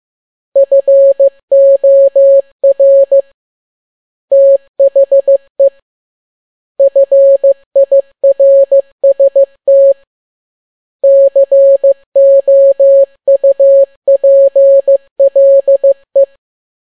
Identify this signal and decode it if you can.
Morse code, heard as a keyed tone, about 15 wpm, 550 Hz. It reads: FOR THE FIRST COUPLE
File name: n6tt p1 15C13